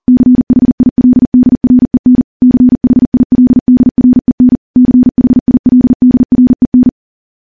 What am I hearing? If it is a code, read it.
Morse code, heard as a keyed tone, about 40 wpm, 250 Hz. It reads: CHILDREN CHILDREN CHILDREN
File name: morse / morse-training-40WPM